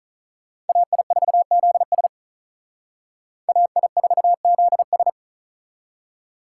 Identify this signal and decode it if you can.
Morse code, heard as a keyed tone, about 39 wpm, 700 Hz. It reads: AI4ZS AI4ZS
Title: Speed Racing - USA 2x2 Call Signs 35wpm